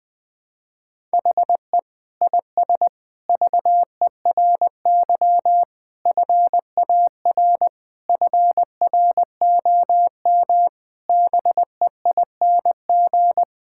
Morse code, heard as a keyed tone, about 20 wpm, 700 Hz. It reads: HE IS VERY FAR FROM BEING
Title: Sentences from Top 1000 Words - Review Part 6 - 20wpm